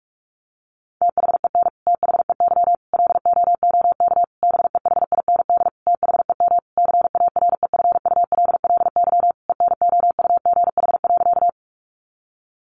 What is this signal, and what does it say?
Morse code, heard as a keyed tone, about 46 wpm, 700 Hz. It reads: THEN THEY LOOK BEHIND THEM CAREFULLY ENOUGH.
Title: Sentences from Top 1300 Words 45wpm